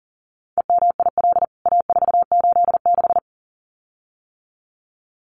Morse code, heard as a keyed tone, about 40 wpm, 700 Hz. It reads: EMIL A486
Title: CWT Exchange 40wpm